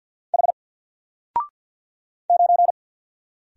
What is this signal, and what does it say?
Morse code, heard as a keyed tone, about 48 wpm, 700 Hz. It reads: H 9
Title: Single Letter-Number Alt 50wpm